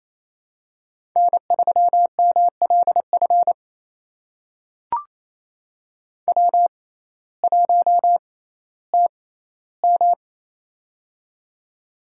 Morse code, heard as a keyed tone, about 28 wpm, spaced as for 15 wpm, 700 Hz. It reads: N3MLF W1TM
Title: USA Call Signs - 6x Character Spacing - 28wpm